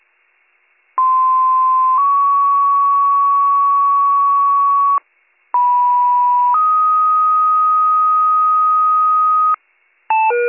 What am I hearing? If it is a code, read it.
Two-tone page, heard as a one-second tone, then a three-second tone, about 1040 then 1120 Hz, about 980 then 1250 Hz. It is heard from 1.0 to 9.6 s.